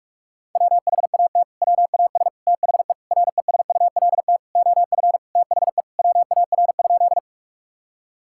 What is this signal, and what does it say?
Morse code, heard as a keyed tone, about 46 wpm, 700 Hz. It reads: WHAT WAS THE RESULT OF THE WAR?
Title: Sentences from Top 1000 Words - Review Part 5 - 45wpm